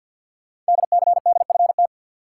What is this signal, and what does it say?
Morse code, heard as a keyed tone, about 52 wpm, 700 Hz. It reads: DCDFT
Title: Code Groups - 5 Letters - 50wpm